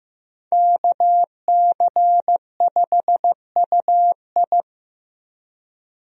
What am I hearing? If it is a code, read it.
Morse code, heard as a keyed tone, about 15 wpm, 700 Hz. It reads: KC5UI
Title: USA Call Signs - 5 Characters - Repeated 3x - 15wpm